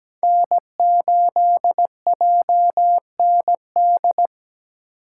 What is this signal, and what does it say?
Morse code, heard as a keyed tone, about 17 wpm, 700 Hz. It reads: N8JND
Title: USA Call Signs - 5 Characters - 3x Character Spacing - 17wpm